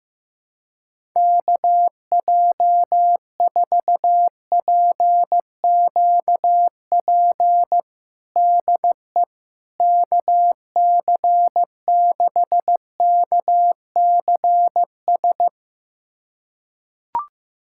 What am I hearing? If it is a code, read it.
Morse code, heard as a keyed tone, about 15 wpm, 700 Hz. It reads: KJ4PQP DE KC6KCS
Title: USA Callsign DE Callsign - 6 Characters - 15wpm